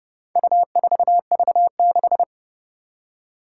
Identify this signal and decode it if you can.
Morse code, heard as a keyed tone, about 30 wpm, 700 Hz. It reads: U4V6
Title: Four Challenging Letter-Number Combinations 30wpm